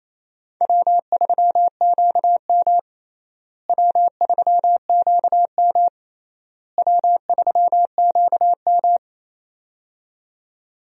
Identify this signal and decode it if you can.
Morse code, heard as a keyed tone, about 28 wpm, 700 Hz. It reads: W3QM W3QM W3QM
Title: USA Call Signs - 4 Characters - Repeated 3x - 28wpm